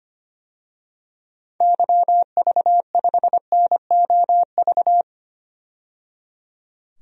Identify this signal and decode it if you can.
Morse code, heard as a keyed tone, about 25 wpm, 690 Hz. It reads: YV5NOV
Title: DXCC-100 Call Signs - 6 Characters - 25wpm